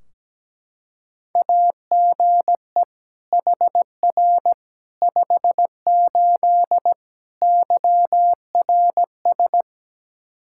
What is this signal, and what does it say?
Morse code, heard as a keyed tone, about 17 wpm, 700 Hz. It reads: AGE HR 58 YRS
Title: Practice QSO - Exchange 3 - More Info - 17wpm